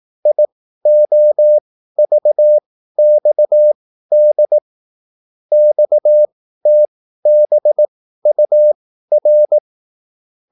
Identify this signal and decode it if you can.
Morse code, heard as a keyed tone, about 18 wpm, spaced as for 13 wpm, 600 Hz. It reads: IOVXD XTBUR